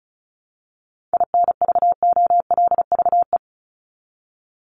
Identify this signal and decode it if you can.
Morse code, heard as a keyed tone, about 34 wpm, 700 Hz. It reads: INVOLVE
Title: ING Suffix - Top 100 Words with Base Words 35wpm